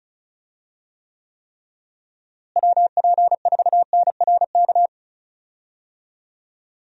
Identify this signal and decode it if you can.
Morse code, heard as a keyed tone, about 36 wpm, 700 Hz. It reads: WP4NRK
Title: USA Call Signs - 6 Characters - Repeated 3x - 35wpm